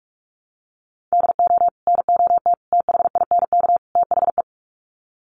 Morse code, heard as a keyed tone, about 45 wpm, 700 Hz. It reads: DO NOT THINK THE
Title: Sets of 4 Words 45wpm